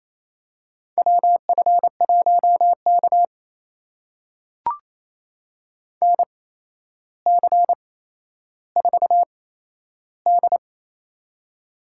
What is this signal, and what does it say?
Morse code, heard as a keyed tone, about 27 wpm, spaced as for 12 wpm, 700 Hz. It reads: WF1K NC4D
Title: USA Call Signs - 4 Characters - 8x Character Spacing - 28wpm